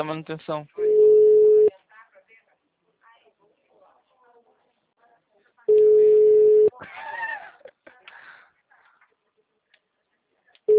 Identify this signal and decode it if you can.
Ringback tone (Europe, 425 Hz), with speech.